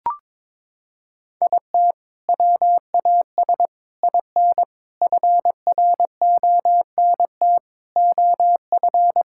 Morse code, heard as a keyed tone, about 22 wpm, 700 Hz. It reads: IT WAS IN FRONT OF
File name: practice / 22wpm / Sentences with 4-Word Phrases - 22wpm